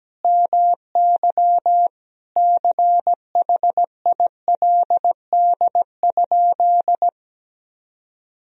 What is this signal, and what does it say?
Morse code, heard as a keyed tone, about 17 wpm, 700 Hz. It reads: MY CHILD?
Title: Sentences from Top 1000 Words - Review Part 4 - 17wpm